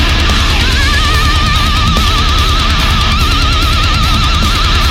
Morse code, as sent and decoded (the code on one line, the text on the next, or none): none